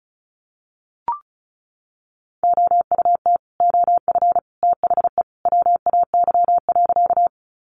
Morse code, heard as a keyed tone, about 34 wpm, 700 Hz. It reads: OUT OF THE WAY.
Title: Sentences from Top 100 Words 35wpm